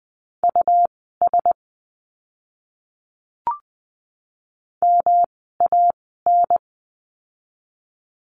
Morse code, heard as a keyed tone, about 20 wpm, spaced as for 15 wpm, 700 Hz. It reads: US MAN
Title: 206 Lesson 33f - Letter j - Words 20wpm